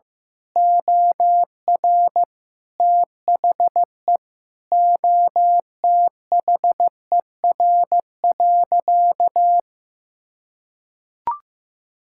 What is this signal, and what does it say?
Morse code, heard as a keyed tone, about 15 wpm, 700 Hz. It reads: OR THE OTHER.
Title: Sentences from Top 1000 Words - Review Part 6 - 15wpm